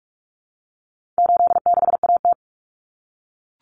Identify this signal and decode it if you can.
Morse code, heard as a keyed tone, about 44 wpm, 700 Hz. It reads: 86AT
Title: Common Cut Numbers - 4-Digit 45wpm